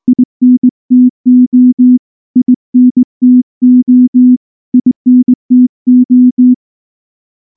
Morse code, heard as a keyed tone, about 19 wpm, 250 Hz. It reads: INTO INTO INTO
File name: morse / morse-training-20WPM-4